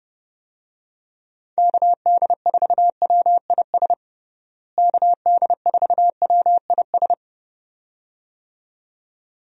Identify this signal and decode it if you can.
Morse code, heard as a keyed tone, about 30 wpm, 700 Hz. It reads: KD4WIS KD4WIS
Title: USA Call Signs - 6 Characters - Repeated 2x - 30wpm